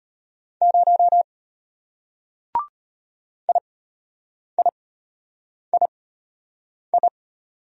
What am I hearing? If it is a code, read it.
Morse code, heard as a keyed tone, about 34 wpm, spaced as for 8 wpm, 700 Hz. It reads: 0 IIII